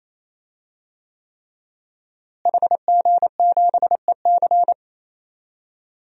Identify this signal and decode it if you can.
Morse code, heard as a keyed tone, about 28 wpm, 700 Hz. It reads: HG7EC
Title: DXCC-100 Call Signs - 5 Characters - Repeated 3x - 28wpm